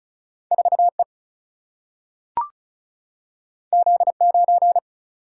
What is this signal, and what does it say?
Morse code, heard as a keyed tone, about 36 wpm, 700 Hz. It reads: 4E Z9
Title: Single-Letter Single-Number 35wpm